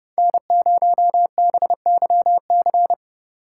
Morse code, heard as a keyed tone, about 30 wpm, 700 Hz. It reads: N0BYC